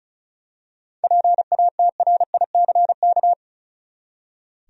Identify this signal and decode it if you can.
Morse code, heard as a keyed tone, about 34 wpm, 700 Hz. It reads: PATRICK
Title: USA Common Names - 35wpm